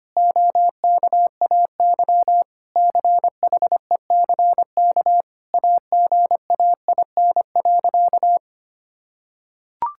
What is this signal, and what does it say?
Morse code, heard as a keyed tone, about 25 wpm, 700 Hz. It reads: OKAY CHECK AGAIN.
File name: practice / 25wpm / Sentences with 4-Word Phrases - 25wpm